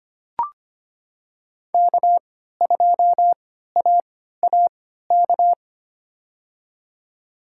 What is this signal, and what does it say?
Morse code, heard as a keyed tone, about 25 wpm, spaced as for 14 wpm, 700 Hz. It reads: K2AAK
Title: USA Call Signs - 5 Characters - 3x Character Spacing - 25wpm